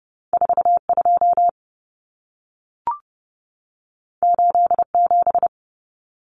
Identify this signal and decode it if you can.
Morse code, heard as a keyed tone, about 30 wpm, 700 Hz. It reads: 42 87
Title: Common Cut Numbers - 2-Digit 30wpm